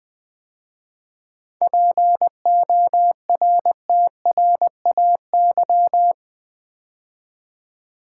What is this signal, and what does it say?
Morse code, heard as a keyed tone, about 20 wpm, 700 Hz. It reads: PORTRAY
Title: 7 Letter Words - First 3 Letters Given - 20wpm